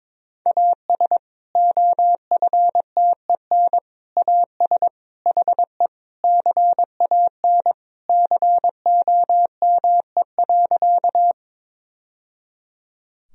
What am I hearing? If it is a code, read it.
Morse code, heard as a keyed tone, about 22 wpm, 700 Hz. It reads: AS OFTEN AS HE CAN COME.
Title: Sentences from Top 1000 Words - Review Part 1 - 22wpm